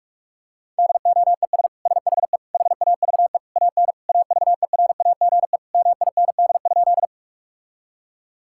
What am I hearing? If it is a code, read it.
Morse code, heard as a keyed tone, about 41 wpm, 700 Hz. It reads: DOES SHE HAVE AN AVERAGE MIND?
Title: Sentences from Top 1000 Words - Review Part 6 - 45wpm